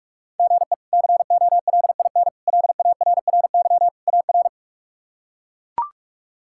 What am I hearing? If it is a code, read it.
Morse code, heard as a keyed tone, about 41 wpm, 700 Hz. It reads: GE COLIN LARRY AR
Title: K1USN SST QSO Practice - 45wpm